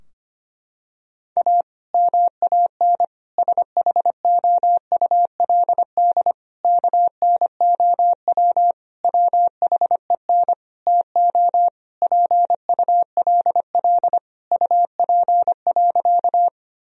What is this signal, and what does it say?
Morse code, heard as a keyed tone, about 25 wpm, 700 Hz. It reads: A MAN SHOULD KNOW WHEN TO PULL UP.
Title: Sentences from Top 1000 Words - Review Part 2 - 25wpm